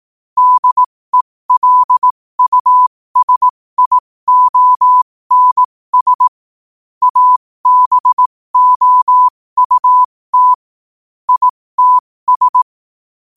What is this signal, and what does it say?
Morse code, heard as a keyed tone, about 18 wpm, spaced as for 15 wpm, 1000 Hz. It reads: DELUSIONS ABOUT ITS
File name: qotdfiles / 15wpm